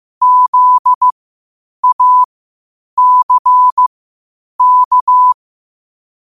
Morse code, heard as a keyed tone, about 15 wpm, spaced as for 8 wpm, 1000 Hz. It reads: ZACK
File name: qotdfiles / 8wpm